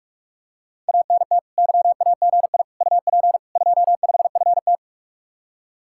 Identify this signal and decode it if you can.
Morse code, heard as a keyed tone, about 48 wpm, 700 Hz. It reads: ANT YAGI UP 25FT